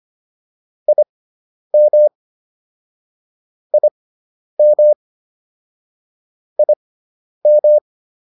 Morse code, heard as a keyed tone, about 25 wpm, spaced as for 10 wpm, 600 Hz. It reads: IM IM IM